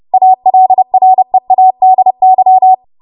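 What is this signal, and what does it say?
Morse code, heard as a keyed tone, about 30 wpm, 750 Hz. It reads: ALREADY
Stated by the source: W1AW Web Code Practice Files, 30 WPM run from ARRL; text from April 2023 QST.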